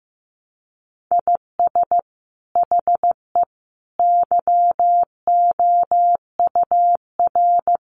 Morse code, heard as a keyed tone, about 15 wpm, 700 Hz. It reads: IS HE YOUR
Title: Sentences from Top 1000 Words 15wpm